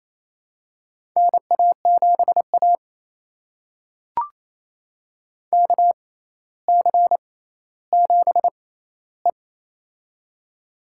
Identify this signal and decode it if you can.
Morse code, heard as a keyed tone, about 28 wpm, spaced as for 14 wpm, 700 Hz. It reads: NA7A KC7E
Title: USA Call Signs - 4 Characters - 6x Character Spacing - 28wpm